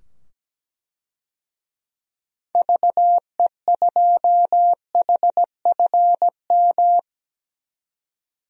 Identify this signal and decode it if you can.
Morse code, heard as a keyed tone, about 17 wpm, 700 Hz. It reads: VE2HFM
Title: DXCC-100 Call Signs - 6 Characters - 17wpm